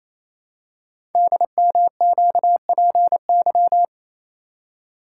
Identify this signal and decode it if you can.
Morse code, heard as a keyed tone, about 29 wpm, 700 Hz. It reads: DMQPY
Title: Code Groups - 5 Letters - 28wpm